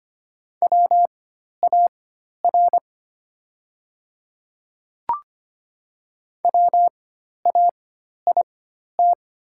Morse code, heard as a keyed tone, about 25 wpm, spaced as for 12 wpm, 700 Hz. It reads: WAR WAIT
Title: 4x Character Spacing - Top 500 Words - 25wpm